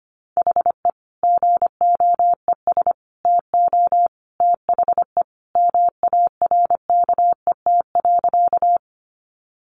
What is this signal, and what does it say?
Morse code, heard as a keyed tone, about 25 wpm, 700 Hz. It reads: HE GOES TO THE MARKET.
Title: Additional Easy Phrases 25wpm